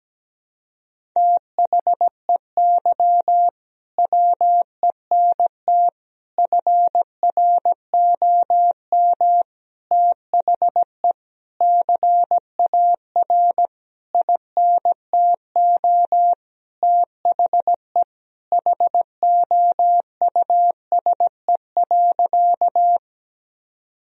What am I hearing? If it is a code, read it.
Morse code, heard as a keyed tone, about 17 wpm, 700 Hz. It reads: THEY WENT FROM THE CAR INTO THE HOUSE.
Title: Sentences from Top 1000 Words - Review Part 1 - 17wpm